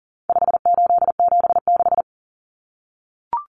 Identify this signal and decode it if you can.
Morse code, heard as a keyed tone, about 37 wpm, 700 Hz. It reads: 5876
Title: Numbers - 4-Digit 40wpm